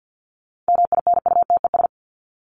Morse code, heard as a keyed tone, about 48 wpm, 700 Hz. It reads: MINUTES
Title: ES:S Suffix - Top 100 Words with Base Words 50wpm